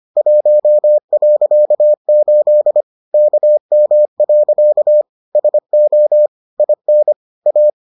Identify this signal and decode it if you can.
Morse code, heard as a keyed tone, about 25 wpm, 600 Hz. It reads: 1.8 KM. SO IN A